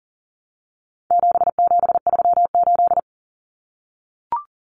Morse code, heard as a keyed tone, about 40 wpm, 700 Hz. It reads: 7738
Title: Numbers - 4-Digit 40wpm